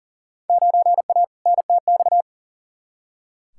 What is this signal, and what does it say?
Morse code, heard as a keyed tone, about 40 wpm, 700 Hz. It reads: 9A NTX
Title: Field Day Exchanges Encore - 40wpm